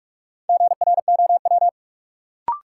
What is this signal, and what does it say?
Morse code, heard as a keyed tone, about 46 wpm, 700 Hz. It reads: GROW
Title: Sentences from Top 300 Words - New Word Drill - 45wpm